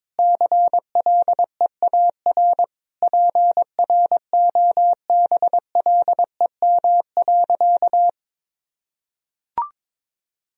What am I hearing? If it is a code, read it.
Morse code, heard as a keyed tone, about 22 wpm, 700 Hz. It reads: CLEAR PROBLEM.
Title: Sentences from Top 1000 Words - Review Part 2 - 22wpm